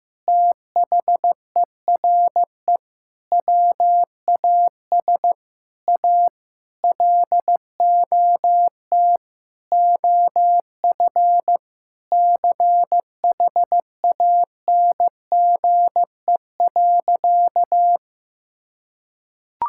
Morse code, heard as a keyed tone, about 15 wpm, 700 Hz. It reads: THERE WAS A LOT OF CHANGE.
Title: Sentences from Top 1000 Words - Review Part 5 - 15wpm